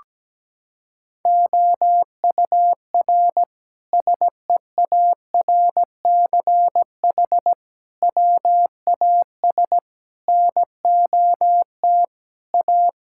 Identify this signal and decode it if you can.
Morse code, heard as a keyed tone, about 17 wpm, 700 Hz. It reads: OUR SEARCH WAS NOT A
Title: Sentences from Top 1000 Words - Review Part 5 - 17wpm